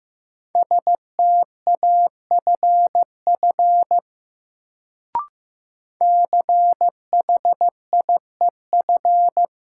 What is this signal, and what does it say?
Morse code, heard as a keyed tone, about 15 wpm, 700 Hz. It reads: STAFF CHIEF
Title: Sets of 5 Words Encore - New Word Drill - 15wpm